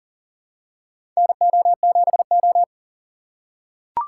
Morse code, heard as a keyed tone, about 40 wpm, 700 Hz. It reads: NO7O
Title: USA Call Signs - 4 Characters - 3x Character Spacing - 40wpm